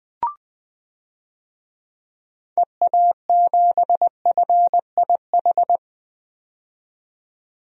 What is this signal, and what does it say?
Morse code, heard as a keyed tone, about 20 wpm, 700 Hz. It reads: EA7FIH
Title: DXCC-100 Call Signs - 6 Characters - 20wpm